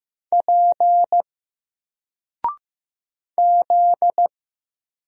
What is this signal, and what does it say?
Morse code, heard as a keyed tone, about 15 wpm, 700 Hz. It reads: P Z
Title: Single Character (Letter:Number:Punctuation:Prosign) 15wpm